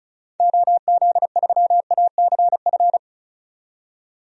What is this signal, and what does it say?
Morse code, heard as a keyed tone, about 36 wpm, 700 Hz. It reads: OZ3ACF
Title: DXCC-100 Call Signs - 6 Characters - 35wpm